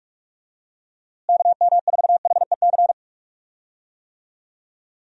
Morse code, heard as a keyed tone, about 46 wpm, 700 Hz. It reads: KM4HEC
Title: USA Call Signs - 6 Characters - Repeated 2x - 45wpm